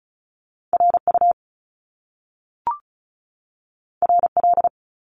Morse code, heard as a keyed tone, about 36 wpm, 700 Hz. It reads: R U R L
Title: Top Two Letter Combinations in Words 35wpm